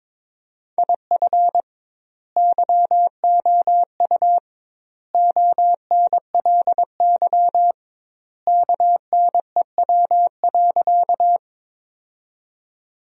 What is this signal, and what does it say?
Morse code, heard as a keyed tone, about 21 wpm, 700 Hz. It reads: IF YOU ONLY KNEW.
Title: 2x Word-Spacing - Sentences from Top 400 Words - 22wpm